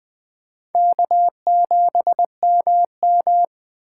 Morse code, heard as a keyed tone, about 21 wpm, 700 Hz. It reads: K7MM
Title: USA Call Signs - 4 Characters - 6x Character Spacing - 20wpm